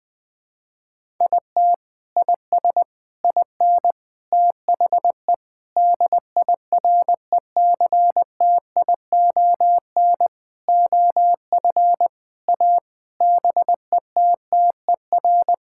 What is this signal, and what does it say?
Morse code, heard as a keyed tone, about 21 wpm, 700 Hz. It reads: IT IS IN THE DIRECTION OF A BETTER
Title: Sentences from Top 1000 Words - Review Part 5 - 20wpm